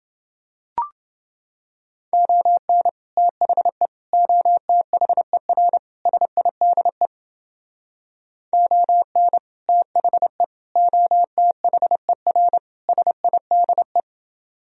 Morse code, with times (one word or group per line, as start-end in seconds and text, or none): 2.1-2.9 ON
3.2-3.9 THE
4.1-5.8 OTHER
6.1-7.1 SIDE
8.5-9.4 ON
9.7-10.4 THE
10.8-12.6 OTHER
12.9-14.0 SIDE